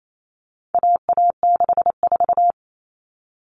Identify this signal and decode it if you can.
Morse code, heard as a keyed tone, about 28 wpm, 700 Hz. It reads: AA64
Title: All Cut Numbers - 4-Digit 28wpm